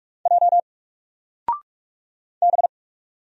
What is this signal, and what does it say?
Morse code, heard as a keyed tone, about 44 wpm, 700 Hz. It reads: J B
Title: Single Letter-Number 45wpm